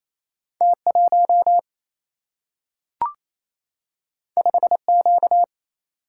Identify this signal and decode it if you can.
Morse code, heard as a keyed tone, about 28 wpm, 700 Hz. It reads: T1 5Q
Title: Single-Letter Single-Number 28wpm